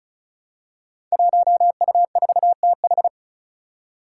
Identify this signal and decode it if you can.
Morse code, heard as a keyed tone, about 36 wpm, 700 Hz. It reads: 1U4TH
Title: Five Challenging Letter-Number Combinations 35wpm